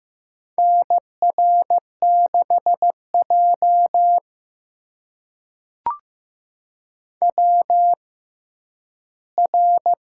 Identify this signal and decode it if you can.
Morse code, heard as a keyed tone, about 15 wpm, spaced as for 10 wpm, 700 Hz. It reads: NR6J WR
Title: USA Call Signs - 4 Characters - 6x Character Spacing - 15wpm